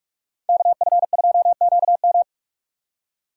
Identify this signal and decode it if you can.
Morse code, heard as a keyed tone, about 47 wpm, 700 Hz. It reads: KF2QM
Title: USA Call Signs - 5 Characters - 4x Character Spacing - 45wpm